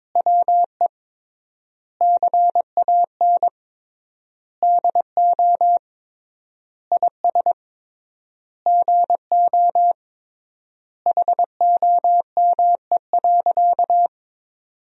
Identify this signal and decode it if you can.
Morse code, heard as a keyed tone, about 22 wpm, 700 Hz. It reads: WE CAN DO IS GO HOME.
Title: 3x Word-Spacing - Sentences from Top 200 Words - 22wpm